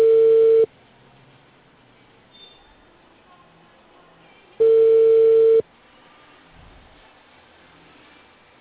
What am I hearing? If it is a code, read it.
Ringback tone (China, 450 Hz).